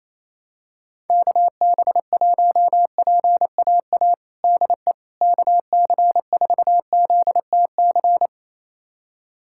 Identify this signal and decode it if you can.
Morse code, heard as a keyed tone, about 28 wpm, 700 Hz. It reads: KB1PAA DE KC4ZTC
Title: USA Callsign DE Callsign - 6 Characters - 28wpm